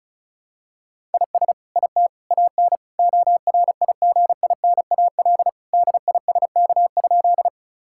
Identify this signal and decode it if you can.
Morse code, heard as a keyed tone, about 34 wpm, 700 Hz. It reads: IS IT AN ORIGINAL DISK?
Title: Sentences from Top 1000 Words - Review Part 6 - 35wpm